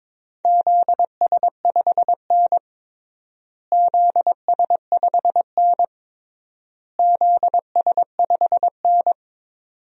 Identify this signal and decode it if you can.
Morse code, heard as a keyed tone, about 22 wpm, 700 Hz. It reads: ZS5N ZS5N ZS5N
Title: DXCC-100 Call Signs - 4 Characters - Repeated 3x - 22wpm